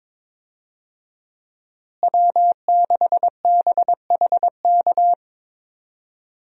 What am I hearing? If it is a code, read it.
Morse code, heard as a keyed tone, about 21 wpm, 700 Hz. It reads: W6BHK
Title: USA Call Signs - 5 Characters - Repeated 2x - 22wpm